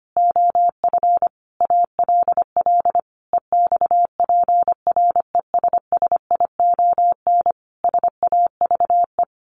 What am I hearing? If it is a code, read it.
Morse code, heard as a keyed tone, about 25 wpm, 700 Hz. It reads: OF ALL EXPRESSION SAVE
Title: A Sound Of Thunder 25wpm